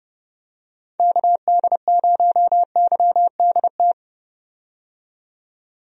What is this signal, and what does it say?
Morse code, heard as a keyed tone, about 30 wpm, 700 Hz. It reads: KD0YDT
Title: USA Call Signs - 6 Characters - 30wpm